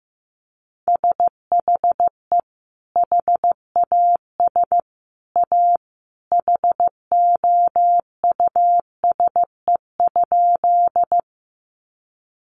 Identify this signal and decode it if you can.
Morse code, heard as a keyed tone, about 15 wpm, 700 Hz. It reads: SHE HAS A HOUSE?
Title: Sentences from Top 200 Words 15wpm